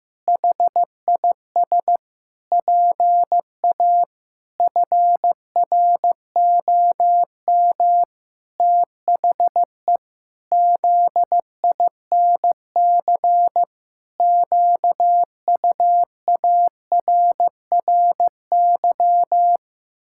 Morse code, heard as a keyed tone, about 16 wpm, 700 Hz. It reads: HIS PA FROM THE ZINC QUARRY
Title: Pangram Challenge v2 - 15wpm